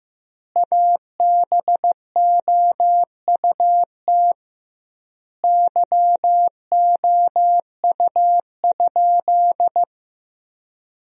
Morse code, heard as a keyed tone, about 15 wpm, 700 Hz. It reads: ABOUT YOU?
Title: 2x Word-Spacing - Sentences from Top 100 Words - 15wpm